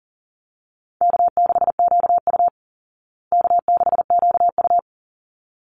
Morse code, heard as a keyed tone, about 40 wpm, 700 Hz. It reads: K6QU K6QU
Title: Easy Call Signs Repeated 2x - 40wpm